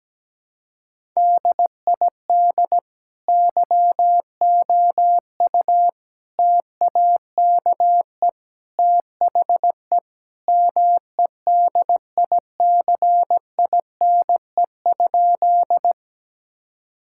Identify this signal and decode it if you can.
Morse code, heard as a keyed tone, about 17 wpm, 700 Hz. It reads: DID YOU TAKE THE MEDICINE?
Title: Sentences from Top 1000 Words - Review Part 6 - 17wpm